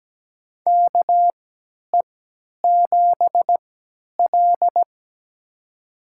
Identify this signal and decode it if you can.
Morse code, heard as a keyed tone, about 17 wpm, spaced as for 10 wpm, 700 Hz. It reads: KE7L
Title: USA Call Signs - 4 Characters - 3x Character Spacing - 17wpm